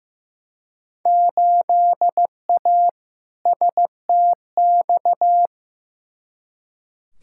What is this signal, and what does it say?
Morse code, heard as a keyed tone, about 15 wpm, 700 Hz. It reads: 8A STX
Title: Field Day Exchanges Encore - 15wpm